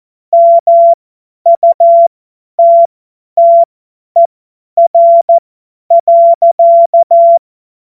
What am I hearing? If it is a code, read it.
Morse code, heard as a keyed tone, about 14 wpm, spaced as for 10 wpm, 680 Hz. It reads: MUTTER.
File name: Morsekurs / Witze / Witz 21. September 2021, 50 F70